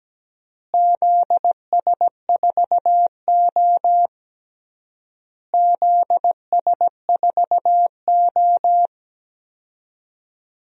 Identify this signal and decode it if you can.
Morse code, heard as a keyed tone, about 17 wpm, 700 Hz. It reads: ZS4O ZS4O
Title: DXCC-100 Call Signs - 4 Characters - Repeated 3x - 17wpm